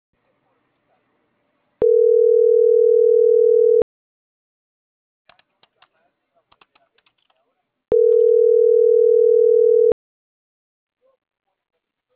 Ringback tone (North America), 440 + 480 Hz.